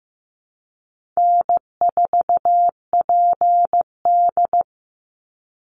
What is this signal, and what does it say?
Morse code, heard as a keyed tone, about 15 wpm, 700 Hz. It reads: N4PD
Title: Call Signs Repeated 2x - 15wpm